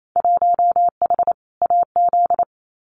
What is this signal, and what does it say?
Morse code, heard as a keyed tone, about 28 wpm, 700 Hz. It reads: 1H AZ
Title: Winter Field Day Exchanges 28wpm